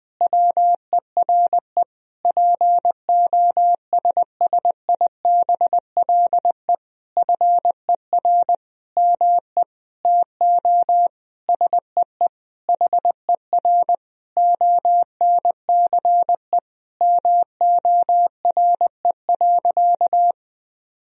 Morse code, heard as a keyed tone, about 20 wpm, 700 Hz. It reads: WERE POSSIBLE FER ME TO SEE HER ONCE MORE.
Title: Sentences from Top 500 Words and Abbreviations 20wpm